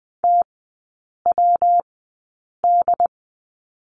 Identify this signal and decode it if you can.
Morse code, heard as a keyed tone, about 19 wpm, spaced as for 8 wpm, 700 Hz. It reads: TWD
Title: 166 Lesson 27e - Number 7 - Three Characters 20wpm